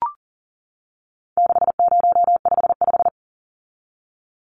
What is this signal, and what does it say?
Morse code, heard as a keyed tone, about 40 wpm, 700 Hz. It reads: 6055
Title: Numbers - 4-Digit 40wpm